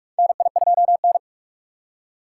Morse code, heard as a keyed tone, about 42 wpm, 700 Hz. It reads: NI2N
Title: USA Call Signs - 4 Characters - 4x Character Spacing - 45wpm